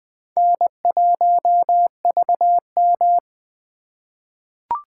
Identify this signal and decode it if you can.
Morse code, heard as a keyed tone, about 20 wpm, 700 Hz. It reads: N1VM
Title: USA Call Signs - 4 Characters - 2x Character Spacing - 20wpm